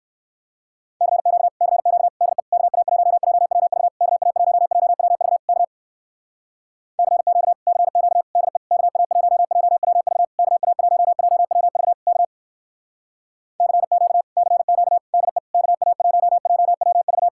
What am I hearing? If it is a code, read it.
Morse code, heard as a keyed tone, about 59 wpm, 700 Hz. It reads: CQ CQ DE KA1JWV KA1JWV K CQ CQ DE KA1JWV KA1JWV K CQ CQ DE KA1JWV